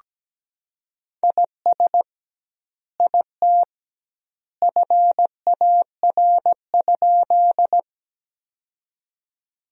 Morse code, heard as a keyed tone, about 17 wpm, 700 Hz. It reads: IS IT FAR?
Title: 2x Word-Spacing - Sentences from Top 300 Words - 17wpm